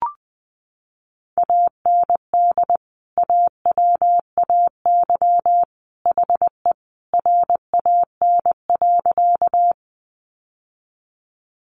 Morse code, heard as a keyed tone, about 20 wpm, 700 Hz. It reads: AND AWAY HE RAN.
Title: Sentences from Top 600 Words 20wpm